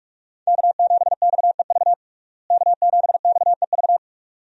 Morse code, heard as a keyed tone, about 47 wpm, 700 Hz. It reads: K7XEV K7XEV
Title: USA Call Signs - 5 Characters - Repeated 2x - 45wpm